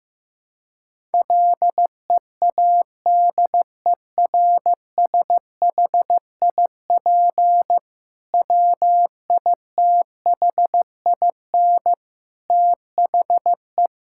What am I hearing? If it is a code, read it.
Morse code, heard as a keyed tone, about 15 wpm, 700 Hz. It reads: LEADERSHIP WITHIN THE
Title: Sentences from Top 1600 Words 15wpm